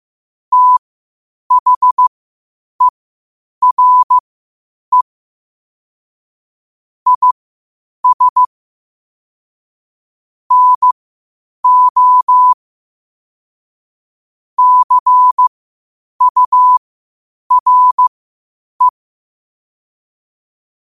Morse code, heard as a keyed tone, about 15 wpm, spaced as for 8 wpm, 1000 Hz. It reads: THERE IS NO CURE
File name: qotdfiles / 8wpm